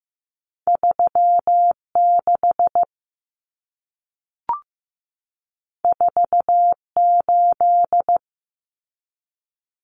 Morse code, heard as a keyed tone, about 15 wpm, 700 Hz. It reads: 36 48
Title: Common Cut Numbers - 2-Digit 15wpm